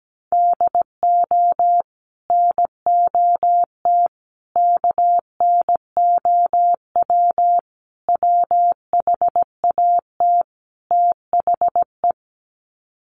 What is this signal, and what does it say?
Morse code, heard as a keyed tone, about 17 wpm, 700 Hz. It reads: DO NOT KNOW WHAT THE
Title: Sets of 5 Words 17wpm